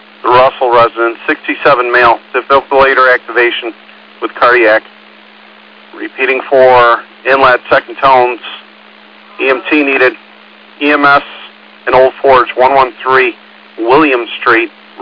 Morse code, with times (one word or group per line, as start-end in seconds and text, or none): none